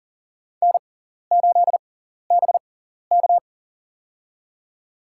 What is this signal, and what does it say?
Morse code, heard as a keyed tone, about 40 wpm, spaced as for 14 wpm, 700 Hz. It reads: N8BK
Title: USA Call Signs - 4 Characters - 6x Character Spacing - 40wpm